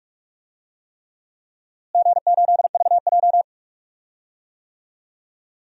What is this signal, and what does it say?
Morse code, heard as a keyed tone, about 46 wpm, 700 Hz. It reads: G8VJ
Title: DXCC-100 Call Signs - 4 Characters - Repeated 3x - 45wpm